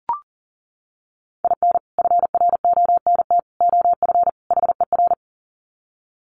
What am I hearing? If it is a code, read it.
Morse code, heard as a keyed tone, about 40 wpm, 700 Hz. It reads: IN FRONT OF HER
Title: Sets of 4 Words 40wpm